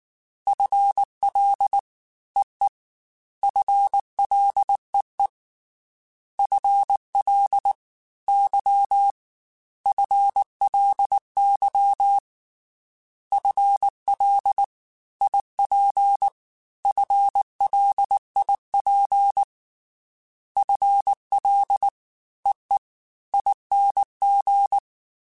Morse code, heard as a keyed tone, about 19 wpm, 800 Hz. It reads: FL EE FLEE FL Y FLY FL IP FLIP FL EE ING
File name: FL / FL mp3s / Fast FL @18